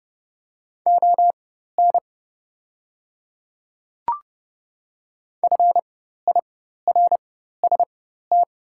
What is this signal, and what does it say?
Morse code, heard as a keyed tone, about 30 wpm, spaced as for 14 wpm, 700 Hz. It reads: ON FIRST
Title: 4x Character Spacing - Top 100 Words - 30wpm